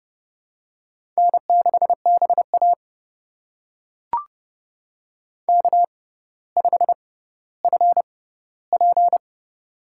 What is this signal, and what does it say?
Morse code, heard as a keyed tone, about 30 wpm, spaced as for 15 wpm, 700 Hz. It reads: N6BA K5FP
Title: USA Call Signs - 4 Characters - 6x Character Spacing - 30wpm